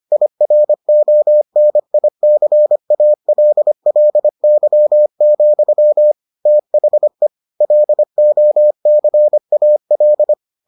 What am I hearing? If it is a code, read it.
Morse code, heard as a keyed tone, about 25 wpm, 600 Hz. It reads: IRONICALLY, THE LOCAL